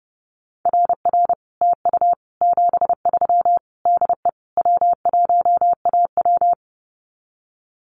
Morse code, heard as a keyed tone, about 30 wpm, 700 Hz. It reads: RR TU 73 DE W1AW
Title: Common QSO Phrases Encore 30wpm